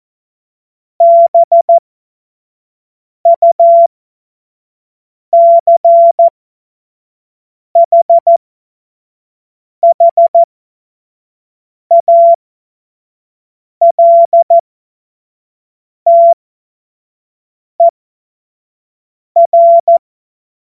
Morse code, heard as a keyed tone, about 14 wpm, spaced as for 5 wpm, 680 Hz. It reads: BUCHHALTER
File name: Morsekurs / Witze / Witz 23. Mai 2023 25 F70